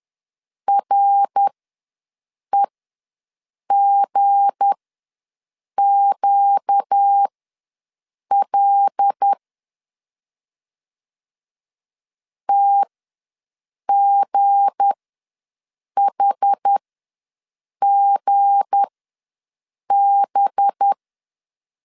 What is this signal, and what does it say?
Morse code, heard as a keyed tone, about 10 wpm, spaced as for 6 wpm, 800 Hz. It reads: REGQL TGHGB
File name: LEZIONI / 7 GF9Z / 7 G ws 10wpm 3843